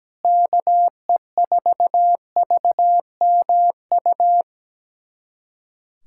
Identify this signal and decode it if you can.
Morse code, heard as a keyed tone, about 17 wpm, 700 Hz. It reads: KE4VMU
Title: USA Callsign DE Callsign - 6 Characters - 17wpm